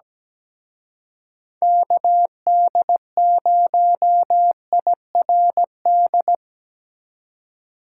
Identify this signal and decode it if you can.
Morse code, heard as a keyed tone, about 20 wpm, 700 Hz. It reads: KD0IRD
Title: Speed Racing - USA 2x3 Call Signs 17wpm